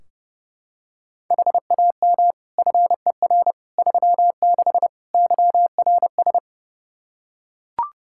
Morse code, heard as a keyed tone, about 30 wpm, 700 Hz. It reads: HAM FER 36 YRS
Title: K1USN SST QSO Practice 30wpm